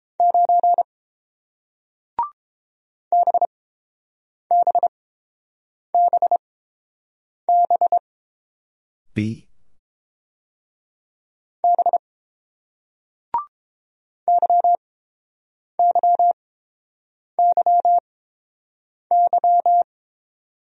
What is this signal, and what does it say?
Morse code, heard as a keyed tone, about 30 wpm, spaced as for 7 wpm, 700 Hz. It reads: 9 BBBB B YYYY